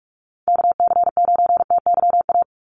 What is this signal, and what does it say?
Morse code, heard as a keyed tone, about 46 wpm, 700 Hz. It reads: KC9TYA
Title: NA SOTA Chaser Callsigns 45wpm